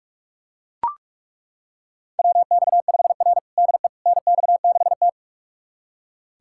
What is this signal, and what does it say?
Morse code, heard as a keyed tone, about 41 wpm, 700 Hz. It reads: WX5R DE NX6T
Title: USA Callsign DE Callsign - 4 Characters - 45wpm